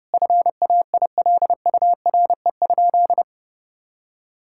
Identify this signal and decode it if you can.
Morse code, heard as a keyed tone, about 30 wpm, 700 Hz. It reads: FAILURE?
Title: Sentences from Top 1000 Words - Review Part 6 - 30wpm